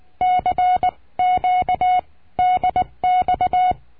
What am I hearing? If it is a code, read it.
Morse code, heard as a keyed tone, about 19 wpm, spaced as for 17 wpm, 730 Hz. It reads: CQ DX
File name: cursos / CURSO DE TELEGRAFIA / Telegrafia - Trilha 22